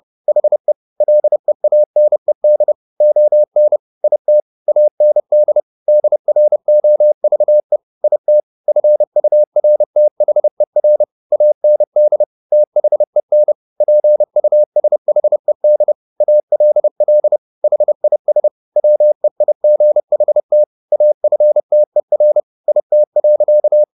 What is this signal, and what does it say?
Morse code, heard as a keyed tone, about 30 wpm, 600 Hz. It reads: HE LEANED ON IT AND DROVE IT FURTHER AND THEN PUSHED ALL HIS WEIGHT AFTER IT.